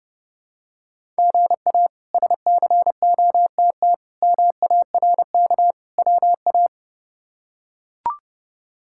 Morse code, heard as a keyed tone, about 30 wpm, 700 Hz. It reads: GA SCOTT MARK WA
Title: K1USN SST QSO Practice - 30wpm